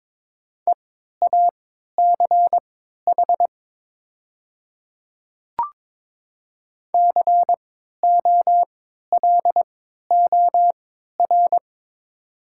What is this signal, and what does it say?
Morse code, heard as a keyed tone, about 22 wpm, spaced as for 13 wpm, 700 Hz. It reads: EACH COLOR
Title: 3x Character Spacing - Top 300 Words - 22wpm